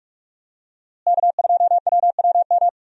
Morse code, heard as a keyed tone, about 46 wpm, 700 Hz. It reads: K2WWM
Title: USA Call Signs - 5 Characters - Repeated 2x - 45wpm